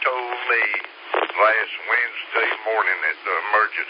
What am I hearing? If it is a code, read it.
Emergency Alert System (SAME) tones, heard as harsh buzzing bursts, with speech. It is heard from 0.5 to 3.2 s.